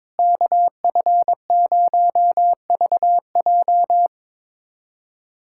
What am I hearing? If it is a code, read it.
Morse code, heard as a keyed tone, about 22 wpm, 700 Hz. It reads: KF0VJ
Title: USA Callsign DE Callsign - 5 Characters - 22wpm